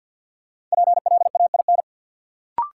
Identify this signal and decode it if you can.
Morse code, heard as a keyed tone, about 52 wpm, 700 Hz. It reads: PLAIN
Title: Sentences from Top 400 Words - New Word Drill - 50wpm